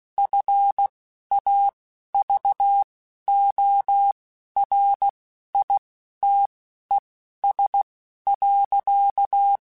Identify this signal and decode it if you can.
Morse code, heard as a keyed tone, about 16 wpm, spaced as for 12 wpm, 800 Hz. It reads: FAVORITES.
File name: QSOs / QSOs-mp3s / WD0FBT 1@10